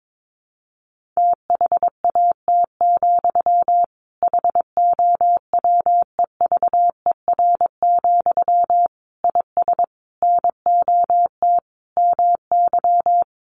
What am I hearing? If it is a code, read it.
Morse code, heard as a keyed tone, about 22 wpm, 700 Hz. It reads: THAT, HOWEVER, IS NOT MY
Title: Sentences from Top 1500 Words 22wpm